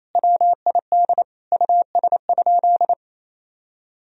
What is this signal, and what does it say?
Morse code, heard as a keyed tone, about 28 wpm, 700 Hz. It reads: WID US?
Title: Sentences from Top 200 Words and Abbreviations 28wpm